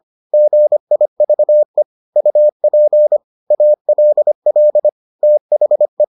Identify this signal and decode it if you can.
Morse code, heard as a keyed tone, about 25 wpm, 600 Hz. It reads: GIVE UP ALL THE